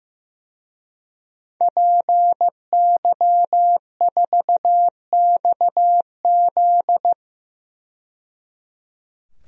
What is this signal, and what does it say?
Morse code, heard as a keyed tone, about 16 wpm, 700 Hz. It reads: PY4XZ
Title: DXCC-100 Call Signs - 5 Characters - 15wpm